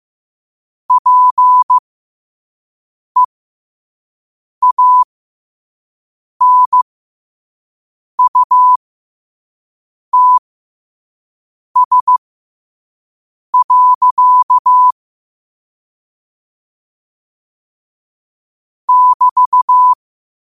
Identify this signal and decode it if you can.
Morse code, heard as a keyed tone, about 15 wpm, spaced as for 5 wpm, 1000 Hz. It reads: PEANUTS. <BT>
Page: New quotes every day in morse code at 5 Words per minute.